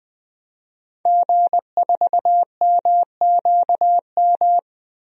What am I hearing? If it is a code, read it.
Morse code, heard as a keyed tone, about 20 wpm, 700 Hz. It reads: G4MQM
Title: DXCC-100 Call Signs - 5 Characters - Repeated 3x - 20wpm